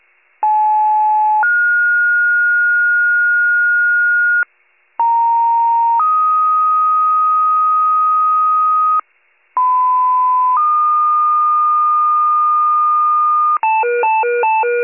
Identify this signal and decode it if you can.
Two-tone page, heard as a one-second tone, then a three-second tone, about 860 then 1430 Hz, about 930 then 1250 Hz, about 1010 then 1220 Hz. It is heard from 0.4 to 13.6 s.
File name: pager / 83 RESCUE DUTY / wcalert